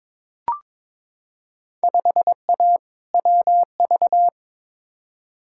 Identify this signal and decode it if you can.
Morse code, heard as a keyed tone, about 22 wpm, 700 Hz. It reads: 5A WV
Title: Field Day Exchanges - 22wpm